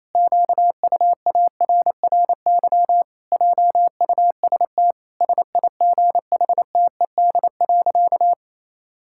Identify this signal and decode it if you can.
Morse code, heard as a keyed tone, about 28 wpm, 700 Hz. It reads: QUARRY JUST SIGHTED.
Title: Pangram Challenge v2 - 28wpm